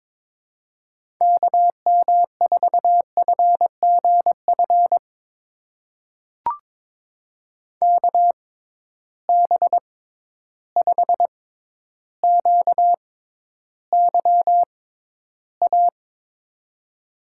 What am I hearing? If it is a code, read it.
Morse code, heard as a keyed tone, about 22 wpm, spaced as for 11 wpm, 700 Hz. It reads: KM4FGF KB5QYA
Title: USA Call Signs - 6 Characters - 6x Character Spacing - 22wpm